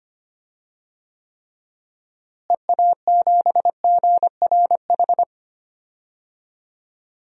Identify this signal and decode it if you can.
Morse code, heard as a keyed tone, about 25 wpm, 700 Hz. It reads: EA7GRH
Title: DXCC-100 Call Signs - 6 Characters - Repeated 2x - 25wpm